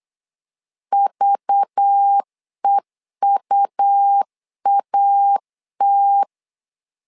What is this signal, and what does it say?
Morse code, heard as a keyed tone, about 8 wpm, 800 Hz. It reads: VEUAT
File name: LEZIONI / 5 LBVC / 5 LBV 8wpm 2837